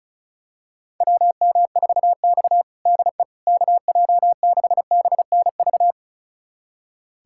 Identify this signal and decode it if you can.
Morse code, heard as a keyed tone, about 34 wpm, 700 Hz. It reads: WM4X DE KJ6BNV
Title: USA Callsign DE Callsign 35wpm